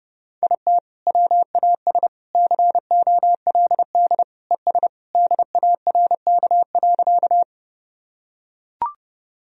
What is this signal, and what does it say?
Morse code, heard as a keyed tone, about 30 wpm, 700 Hz. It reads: IT WAS COLD ES DARK.
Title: Sentences from Top 500 Words and Abbreviations 30wpm